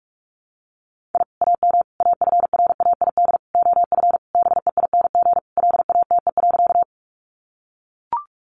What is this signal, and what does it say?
Morse code, heard as a keyed tone, about 44 wpm, 700 Hz. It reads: I AM AFRAID OF BEING LATE.
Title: Sentences from Top 1500 Words 45wpm